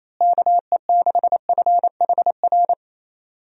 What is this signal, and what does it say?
Morse code, heard as a keyed tone, about 28 wpm, 700 Hz. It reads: KE6FHR
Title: USA Call Signs - 6 Characters - Repeated 2x - 28wpm